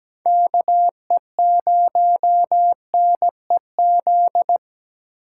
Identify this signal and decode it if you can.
Morse code, heard as a keyed tone, about 20 wpm, 700 Hz. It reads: KE0NEZ
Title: Speed Racing - USA 2x3 Call Signs 17wpm